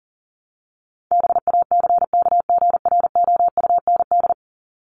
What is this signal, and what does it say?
Morse code, heard as a keyed tone, about 40 wpm, 700 Hz. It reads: BACKGROUND
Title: Top 1001-1250 Words 40wpm